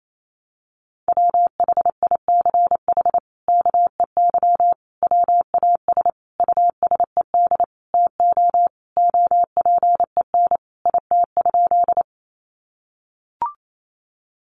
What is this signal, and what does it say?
Morse code, heard as a keyed tone, about 28 wpm, 700 Hz. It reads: WHICH KEY WAS USED TO OPEN IT?
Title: Sentences from Top 600 Words 28wpm